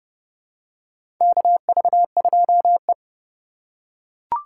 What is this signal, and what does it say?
Morse code, heard as a keyed tone, about 30 wpm, 700 Hz. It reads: KV2E
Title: USA Call Signs - 4 Characters - 8x Character Spacing - 30wpm